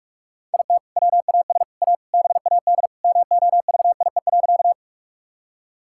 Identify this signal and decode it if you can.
Morse code, heard as a keyed tone, about 46 wpm, 700 Hz. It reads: IT WAS A BAD MOVIE.
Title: Sentences from Top 1000 Words - Review Part 5 - 45wpm